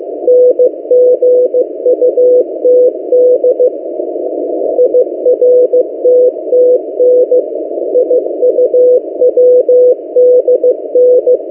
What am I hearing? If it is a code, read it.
Morse code, heard as a keyed tone, about 15 wpm, 500 Hz. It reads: NGUTD IRTTN IUWDN
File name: M08a- -1355utc-8096khz